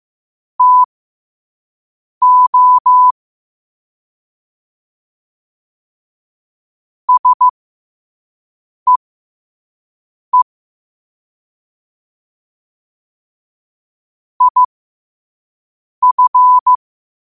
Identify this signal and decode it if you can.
Morse code, heard as a keyed tone, about 15 wpm, spaced as for 5 wpm, 1000 Hz. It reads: TO SEE IF
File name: qotdfiles / 5wpm